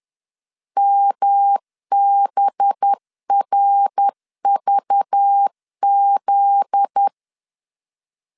Morse code, heard as a keyed tone, about 10 wpm, 800 Hz. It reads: MBRVZ
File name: LEZIONI / 7 GF9Z / 7 GF9Z 10wpm 2611